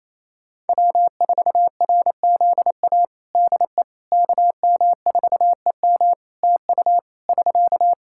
Morse code, heard as a keyed tone, about 28 wpm, 700 Hz. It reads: W4RZA DE KM4EM TU <SK>
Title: Practice QSO - Exchange 4 - Ending - 28wpm